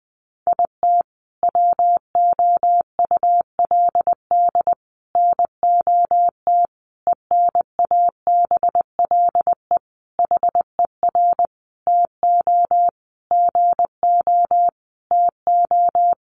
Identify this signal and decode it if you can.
Morse code, heard as a keyed tone, about 19 wpm, 700 Hz. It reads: IT WOULD NOT ENABLE HER TO GO TO
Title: Sentences from Top 1500 Words 20wpm